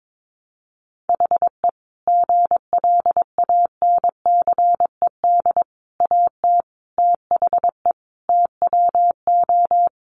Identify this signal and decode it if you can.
Morse code, heard as a keyed tone, about 22 wpm, 700 Hz. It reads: HE GLANCED AT THE TWO
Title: A Sound Of Thunder 22wpm